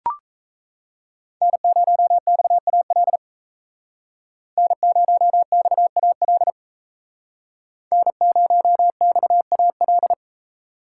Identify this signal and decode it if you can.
Morse code, heard as a keyed tone, about 33 wpm, 700 Hz. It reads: N0XAL N0XAL N0XAL
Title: Speed Racing - USA 1x3 Call Signs 28wpm